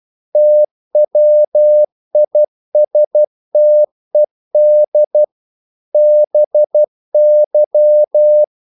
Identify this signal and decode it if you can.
Morse code, heard as a keyed tone, about 12 wpm, 600 Hz. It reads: TWISTED BY
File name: CW / Practice / If12